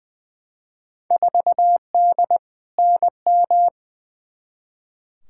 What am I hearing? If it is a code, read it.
Morse code, heard as a keyed tone, about 20 wpm, 700 Hz. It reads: 4D NM
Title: Field Day Exchanges Encore - 20wpm